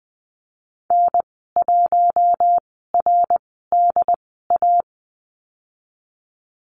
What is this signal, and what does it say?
Morse code, heard as a keyed tone, about 20 wpm, spaced as for 15 wpm, 700 Hz. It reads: N1RDA
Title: 186 Lesson 30g - Letter b - Call Signs 20wpm